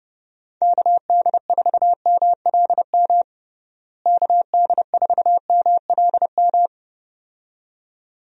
Morse code, heard as a keyed tone, about 30 wpm, 700 Hz. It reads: KD4MLM KD4MLM
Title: USA Call Signs - 6 Characters - Repeated 2x - 30wpm